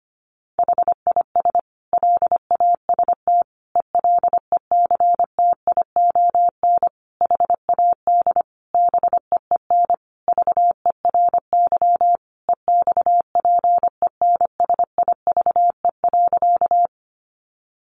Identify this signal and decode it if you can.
Morse code, heard as a keyed tone, about 25 wpm, 700 Hz. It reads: HIS LAST ELECTION HAD BEEN VERY EXPENSIVE.
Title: Sentences from Top 1500 Words 25wpm